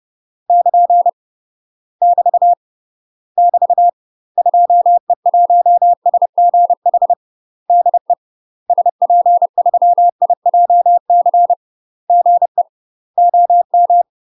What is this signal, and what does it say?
Morse code, heard as a keyed tone, about 30 wpm, 700 Hz. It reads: <KN> <BT> <BT> 2E1SGH DE SP3IJC GE OM
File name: UK MASTER QSOs TRAINING / 30WPM / UK MASTER QSOs 0035